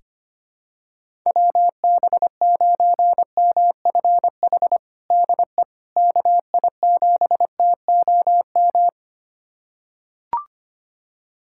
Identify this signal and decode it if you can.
Morse code, heard as a keyed tone, about 25 wpm, 700 Hz. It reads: WB9MFH DE KI7TOM
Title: USA Callsign DE Callsign - 6 Characters - 25wpm